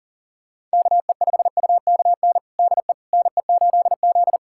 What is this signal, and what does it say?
Morse code, heard as a keyed tone, about 37 wpm, 700 Hz. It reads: KE5UKN DE NE8Z
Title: USA Callsign DE Callsign 40wpm